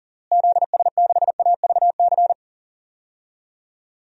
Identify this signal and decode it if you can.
Morse code, heard as a keyed tone, about 40 wpm, 700 Hz. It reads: ZS6AVC
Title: DXCC-100 Call Signs - 6 Characters - Repeated 3x - 40wpm